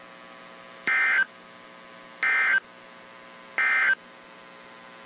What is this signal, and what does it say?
Emergency Alert System (SAME) tones, heard as harsh buzzing bursts. It is heard from 0.8 to 4.0 s.
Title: WZ2523 - NOAA Weather Radio